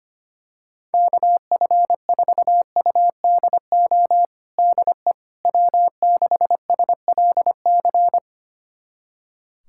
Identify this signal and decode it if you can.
Morse code, heard as a keyed tone, about 25 wpm, 700 Hz. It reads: KF4UDO DE W6SLC
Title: USA Callsign DE Callsign 25wpm